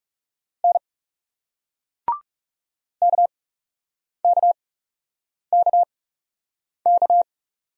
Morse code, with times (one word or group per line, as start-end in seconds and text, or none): 0.6-0.8 N
3.0-7.2 KKKK